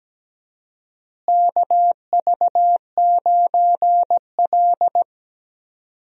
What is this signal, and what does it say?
Morse code, heard as a keyed tone, about 17 wpm, 700 Hz. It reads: KV9L
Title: USA Call Signs - 4 Characters - Repeated 2x - 17wpm